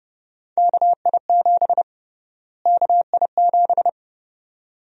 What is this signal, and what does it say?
Morse code, heard as a keyed tone, about 30 wpm, 700 Hz. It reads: KI7 KI7
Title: USA Call Sign Prefixes - Repeated 2x - 30wpm